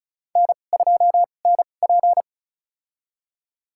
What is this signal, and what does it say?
Morse code, heard as a keyed tone, about 36 wpm, spaced as for 25 wpm, 700 Hz. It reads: N2NP
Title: USA Call Signs - 4 Characters - 2x Character Spacing - 35wpm